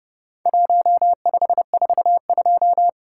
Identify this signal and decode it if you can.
Morse code, heard as a keyed tone, about 30 wpm, 700 Hz. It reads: 1542
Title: Worked All Europe DX Contest - QTC Exchange - 30wpm